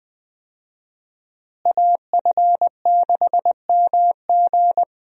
Morse code, heard as a keyed tone, about 20 wpm, 700 Hz. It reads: AF6MG
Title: USA Call Signs - 5 Characters - Repeated 3x - 20wpm